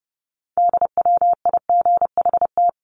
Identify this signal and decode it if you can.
Morse code, heard as a keyed tone, about 30 wpm, 700 Hz. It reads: DWIGHT
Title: CWT Exchange 30wpm